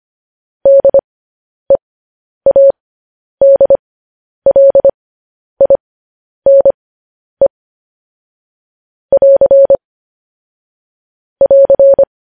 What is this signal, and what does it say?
Morse code, heard as a keyed tone, about 25 wpm, spaced as for 10 wpm, 550 Hz. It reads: DEADLINE <AR> <AR>
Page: news-headlines-in-morse-code-at-10-wpm